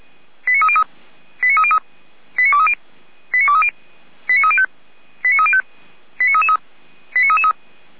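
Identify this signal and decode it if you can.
Five-tone selective call (ZVEI), heard as quick runs of five tones, tones of about 70 ms. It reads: 89303 89303 89230 89230 89305 89305 89333 89333
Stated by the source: ZVEI five-tone selective call received by SDR, from the Feuerwehr Orlamünde fire brigade.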